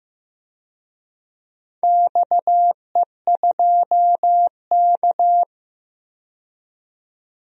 Morse code, heard as a keyed tone, about 15 wpm, 700 Hz. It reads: XE2K
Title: DXCC-100 Call Signs - 4 Characters - 15wpm